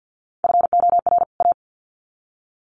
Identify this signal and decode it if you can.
Morse code, heard as a keyed tone, about 50 wpm, 700 Hz. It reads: FOR A
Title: Sets of 2 Words 50wpm